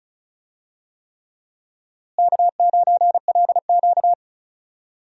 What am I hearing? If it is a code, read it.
Morse code, heard as a keyed tone, about 34 wpm, 700 Hz. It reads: K9LQ
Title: USA Call Signs - 4 Characters - Repeated 3x - 35wpm